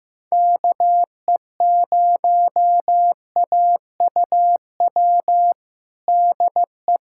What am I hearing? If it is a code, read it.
Morse code, heard as a keyed tone, about 15 wpm, 700 Hz. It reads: KE0AUW DE
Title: USA Callsign DE Callsign - 6 Characters - 15wpm